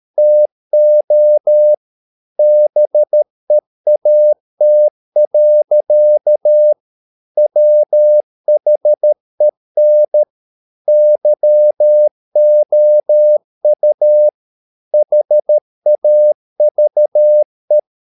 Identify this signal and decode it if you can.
Morse code, heard as a keyed tone, about 13 wpm, 600 Hz. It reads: TO BEAT. WHEN YOU HAVE
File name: mp3 files / Lexiphiles - 13wpm